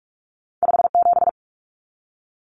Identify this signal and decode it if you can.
Morse code, heard as a keyed tone, about 46 wpm, 700 Hz. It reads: 57
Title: Common Cut Numbers - 2-Digit 45wpm